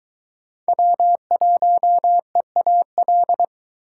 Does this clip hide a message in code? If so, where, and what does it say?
Yes: Morse code from 0.7 to 3.4 s, heard as a keyed tone, about 20 wpm, 700 Hz. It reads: W1EAL